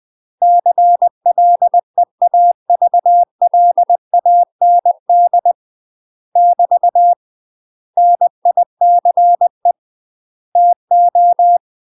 Morse code, heard as a keyed tone, about 20 wpm, 700 Hz. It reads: CLEAVLAND <BT> NICE TO
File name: UK MASTER QSOs TRAINING/20WPM